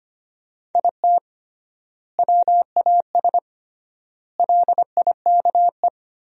Morse code, heard as a keyed tone, about 25 wpm, 700 Hz. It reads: IT WAS LIKE
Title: 3x Word-Spacing - Sentences from Top 300 Words - 25wpm